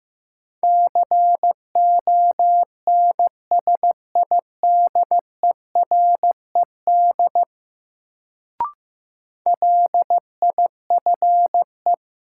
Morse code, heard as a keyed tone, about 15 wpm, 700 Hz. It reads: CONSIDERED LIFE
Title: Sets of 5 Words - New Word Drill - 15wpm